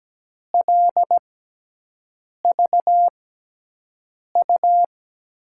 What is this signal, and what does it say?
Morse code, heard as a keyed tone, about 17 wpm, spaced as for 6 wpm, 700 Hz. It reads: LVU
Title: USA Call Signs - 6 Characters - 6x Character Spacing - 17wpm